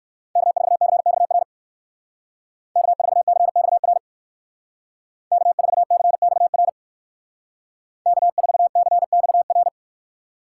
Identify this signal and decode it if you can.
Morse code, heard as a keyed tone, about 59 wpm, 700 Hz. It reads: K4CXR K4CXR K4CXR K4CXR